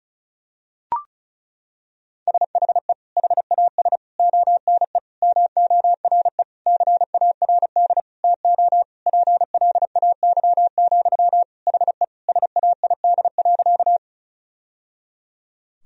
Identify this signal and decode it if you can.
Morse code, heard as a keyed tone, about 36 wpm, 700 Hz. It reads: SHE HAS ONE MORE CARD TO PLAY, HE SAID.
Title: Sentences from Top 1000 Words - Review Part 4 - 35wpm